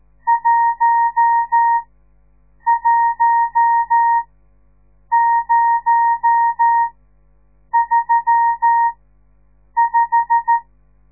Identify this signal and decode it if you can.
Morse code, heard as a keyed tone, about 13 wpm, spaced as for 7 wpm, 930 Hz. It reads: 11035
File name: radiomorningstar / morse / les 10